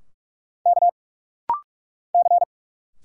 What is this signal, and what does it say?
Morse code, heard as a keyed tone, about 46 wpm, 700 Hz. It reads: K C
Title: Single Letters - Rapid-Fire 45wpm